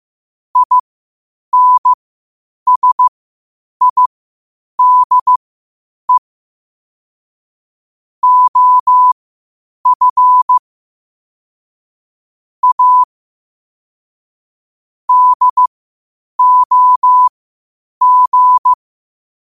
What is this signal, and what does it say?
Morse code, heard as a keyed tone, about 15 wpm, spaced as for 8 wpm, 1000 Hz. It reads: INSIDE OF A DOG
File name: qotdfiles / 8wpm